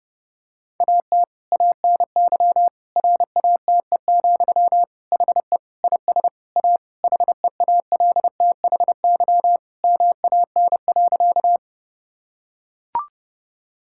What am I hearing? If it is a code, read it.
Morse code, heard as a keyed tone, about 30 wpm, 700 Hz. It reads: AT ANY RATE, HE IS A HEALTHY MAN.
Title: Sentences from Top 1000 Words - Review Part 5 - 30wpm